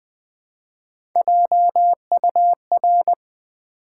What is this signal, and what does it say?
Morse code, heard as a keyed tone, about 20 wpm, 700 Hz. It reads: JUR
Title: Build A Word - 1 Letter At A Time - 3 to 5-Letter Words - 20wpm